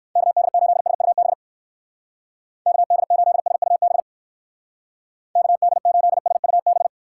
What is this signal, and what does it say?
Morse code, heard as a keyed tone, about 59 wpm, 700 Hz. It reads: KD8SUB KD8SUB KD8SUB